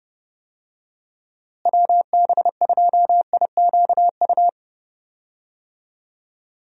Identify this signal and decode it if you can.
Morse code, heard as a keyed tone, about 30 wpm, 700 Hz. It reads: WB2IQU
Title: USA Call Signs - 6 Characters - Repeated 3x - 30wpm